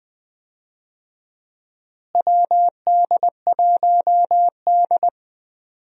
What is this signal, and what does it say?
Morse code, heard as a keyed tone, about 20 wpm, 700 Hz. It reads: WD1D
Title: USA Call Signs - 4 Characters - Repeated 3x - 20wpm